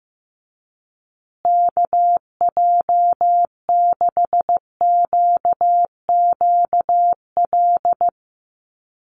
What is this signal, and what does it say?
Morse code, heard as a keyed tone, about 15 wpm, 700 Hz. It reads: KJ6QQL
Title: USA Call Signs Repeated 3x 15wpm